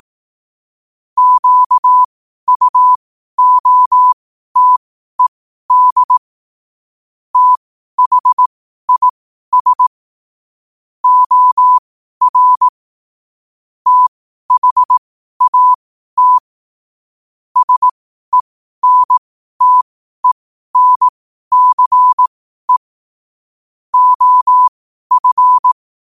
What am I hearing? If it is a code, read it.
Morse code, heard as a keyed tone, about 18 wpm, spaced as for 12 wpm, 1000 Hz. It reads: QUOTED THIS OR THAT SENTENCE OF
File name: qotdfiles / 12wpm